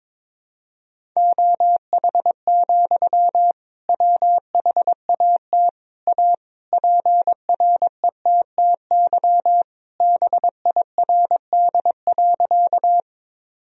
Morse code, heard as a keyed tone, about 22 wpm, 700 Hz. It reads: OH, WHAT A PRETTY BIRD.
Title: Sentences from Top 1000 Words - Review Part 2 - 22wpm